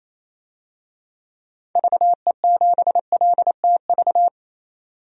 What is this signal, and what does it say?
Morse code, heard as a keyed tone, about 28 wpm, 700 Hz. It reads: VE7LTV
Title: DXCC-100 Call Signs - 6 Characters - 28wpm